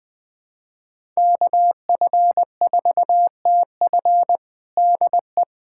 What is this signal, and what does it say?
Morse code, heard as a keyed tone, about 20 wpm, 700 Hz. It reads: KF4TF DE
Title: USA Callsign DE Callsign - 5 Characters - 20wpm